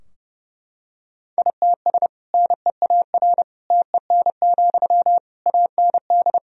Morse code, heard as a keyed tone, about 30 wpm, 700 Hz. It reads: ITS NEAR TEN, AND
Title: Sentences from Top 1000 Words - Review Part 2 - 30wpm